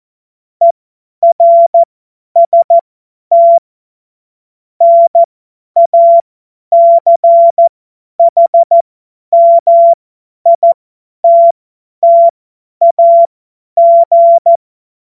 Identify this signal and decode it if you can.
Morse code, heard as a keyed tone, about 14 wpm, spaced as for 10 wpm, 680 Hz. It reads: ERST NACHMITTAG
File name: Morsekurs / Witze / Witz 25. April 2023 50 F70